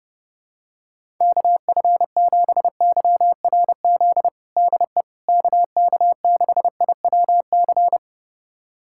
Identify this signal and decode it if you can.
Morse code, heard as a keyed tone, about 30 wpm, 700 Hz. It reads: KF7YRZ DE KK6IWC
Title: USA Callsign DE Callsign - 6 Characters - 30wpm